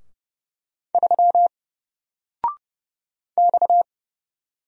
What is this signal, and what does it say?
Morse code, heard as a keyed tone, about 29 wpm, 700 Hz. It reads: 3 X
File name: Single-Letter-Number-30wpm